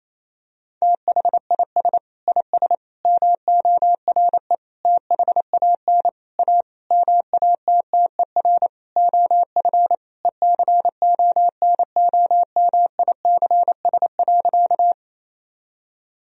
Morse code, heard as a keyed tone, about 28 wpm, 700 Hz. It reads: THIS IS MORE THAN A MATTER OF ECONOMICS.
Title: Sentences from Top 1000 Words - Review Part 3 - 28wpm